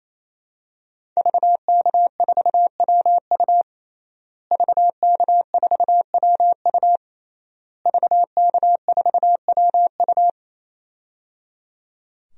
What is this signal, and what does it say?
Morse code, heard as a keyed tone, about 28 wpm, 700 Hz. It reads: VK4WU VK4WU VK4WU
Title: DXCC-100 Call Signs - 5 Characters - Repeated 3x - 28wpm